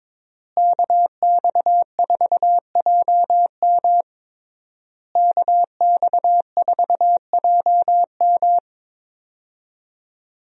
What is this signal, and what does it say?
Morse code, heard as a keyed tone, about 22 wpm, 700 Hz. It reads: KX4JM KX4JM
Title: USA Call Signs - 5 Characters - Repeated 3x - 22wpm